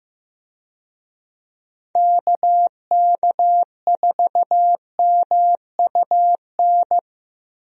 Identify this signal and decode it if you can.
Morse code, heard as a keyed tone, about 15 wpm, 700 Hz. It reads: KK4MUN
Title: USA Call Signs - 6 Characters - Repeated 2x - 15wpm